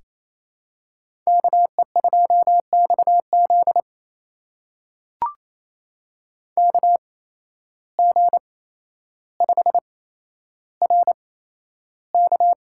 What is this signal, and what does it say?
Morse code, heard as a keyed tone, about 28 wpm, spaced as for 12 wpm, 700 Hz. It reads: KE2XZ KG5RK
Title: USA Call Signs - 5 Characters - 8x Character Spacing - 28wpm